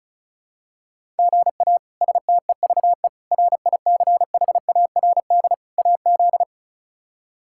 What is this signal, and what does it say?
Morse code, heard as a keyed tone, about 34 wpm, 700 Hz. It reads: GA STEVE RICHARD AZ
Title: K1USN SST QSO Practice - 35wpm